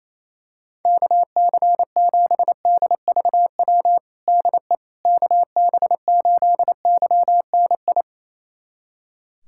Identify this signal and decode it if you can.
Morse code, heard as a keyed tone, about 28 wpm, 700 Hz. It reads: KC7DVW DE KB8YNI
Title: USA Callsign DE Callsign 28wpm